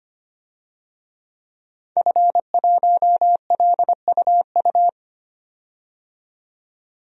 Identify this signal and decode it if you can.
Morse code, heard as a keyed tone, about 25 wpm, 700 Hz. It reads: F1LUU
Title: DXCC-100 Call Signs - 5 Characters - 25wpm